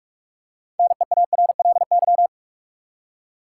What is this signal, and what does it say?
Morse code, heard as a keyed tone, about 42 wpm, 700 Hz. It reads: NEARLY
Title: Six Letter Words - 45wpm